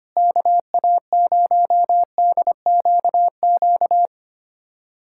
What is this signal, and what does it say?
Morse code, heard as a keyed tone, about 25 wpm, 700 Hz. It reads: KA0DQQ
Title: USA Callsign DE Callsign - 6 Characters - 25wpm